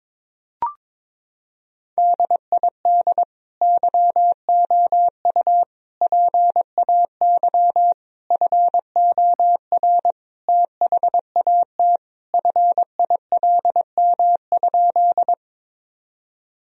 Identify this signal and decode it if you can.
Morse code, heard as a keyed tone, about 22 wpm, 700 Hz. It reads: DID YOU PAY FOR THAT FILM?
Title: Sentences from Top 1000 Words - Review Part 3 - 22wpm